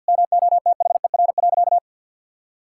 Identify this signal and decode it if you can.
Morse code, heard as a keyed tone, about 50 wpm, 700 Hz. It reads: MOTHER.
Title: Sentences from Top 1000 Words - Review Part 6 - 50wpm